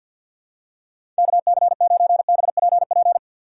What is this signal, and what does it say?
Morse code, heard as a keyed tone, about 50 wpm, 700 Hz. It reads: KC9BPP
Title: USA Call Signs - 6 Characters - 4x Character Spacing - 50wpm